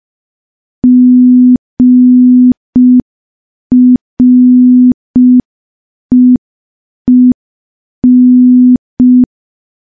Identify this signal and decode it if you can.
Morse code, heard as a keyed tone, about 5 wpm, 250 Hz. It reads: GREEN
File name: morse / morse-training-05WPM